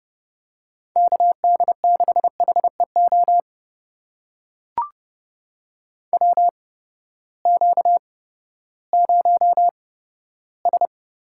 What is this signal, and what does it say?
Morse code, heard as a keyed tone, about 30 wpm, spaced as for 15 wpm, 700 Hz. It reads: KD6HEO WQ0S
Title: USA Call Signs - 8x Character Spacing - 30wpm